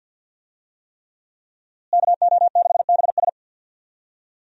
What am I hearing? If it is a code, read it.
Morse code, heard as a keyed tone, about 52 wpm, 700 Hz. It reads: KO6BS
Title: USA Call Signs - 5 Characters - Repeated 3x - 50wpm